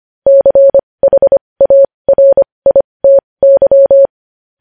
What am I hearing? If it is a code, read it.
Morse code, heard as a keyed tone, about 25 wpm, spaced as for 20 wpm, 550 Hz. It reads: CHARITY
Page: News headlines in morse code at 20 wpm